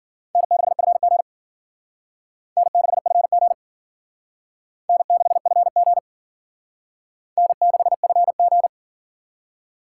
Morse code, heard as a keyed tone, about 52 wpm, 700 Hz. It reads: N6FG N6FG N6FG N6FG